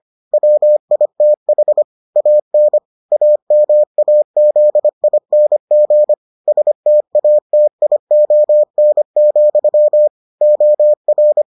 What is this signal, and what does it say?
Morse code, heard as a keyed tone, about 25 wpm, 600 Hz. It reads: WITH AN AMAZING STATION, OR